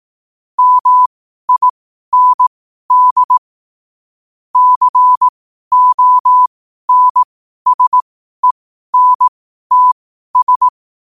Morse code, heard as a keyed tone, about 18 wpm, spaced as for 12 wpm, 1000 Hz. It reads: MIND CONSENTS
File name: qotdfiles / 12wpm